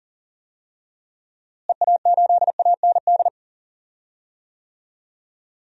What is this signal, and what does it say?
Morse code, heard as a keyed tone, about 40 wpm, 700 Hz. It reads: EA8AND
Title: DXCC-100 Call Signs - 6 Characters - Repeated 2x - 40wpm